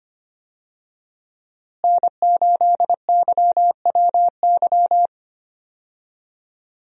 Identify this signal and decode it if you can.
Morse code, heard as a keyed tone, about 25 wpm, 700 Hz. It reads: N8YWY
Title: USA Call Signs - 5 Characters - Repeated 3x - 25wpm